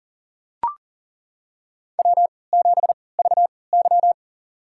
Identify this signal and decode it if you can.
Morse code, heard as a keyed tone, about 37 wpm, spaced as for 22 wpm, 700 Hz. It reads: W7VY
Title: USA Call Signs - 3x Character Spacing - 40wpm